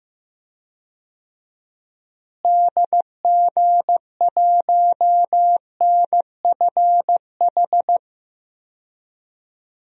Morse code, heard as a keyed tone, about 15 wpm, 700 Hz. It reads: DG1NFH
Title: DXCC-100 Call Signs - 6 Characters - Repeated 3x - 15wpm